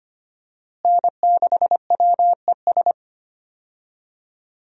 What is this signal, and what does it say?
Morse code, heard as a keyed tone, about 25 wpm, 700 Hz. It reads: N6WES
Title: USA Call Signs - 5 Characters - Repeated 3x - 25wpm